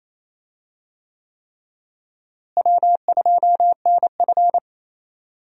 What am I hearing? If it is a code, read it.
Morse code, heard as a keyed tone, about 28 wpm, 700 Hz. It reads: W2NF
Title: USA Call Signs - 4 Characters - Repeated 3x - 28wpm